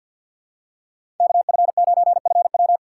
Morse code, heard as a keyed tone, about 53 wpm, 700 Hz. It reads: KF9FW
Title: USA Call Signs - 5 Characters - 2x Character Spacing - 50wpm